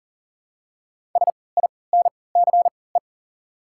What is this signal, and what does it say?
Morse code, heard as a keyed tone, about 40 wpm, spaced as for 23 wpm, 700 Hz. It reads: SINCE
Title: 3x Character Spacing - Top 500 Words - 40wpm